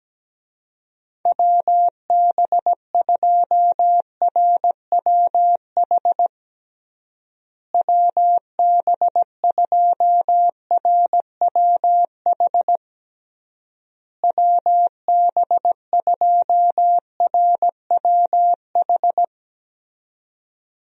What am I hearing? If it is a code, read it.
Morse code, heard as a keyed tone, about 17 wpm, 700 Hz. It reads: WB2RWH WB2RWH WB2RWH
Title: USA Call Signs - 6 Characters - Repeated 3x - 17wpm